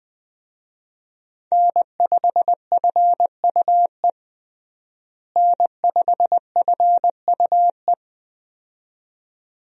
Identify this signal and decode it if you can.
Morse code, heard as a keyed tone, about 21 wpm, 700 Hz. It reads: N5FUE N5FUE
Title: USA Call Signs - 5 Characters - Repeated 2x - 20wpm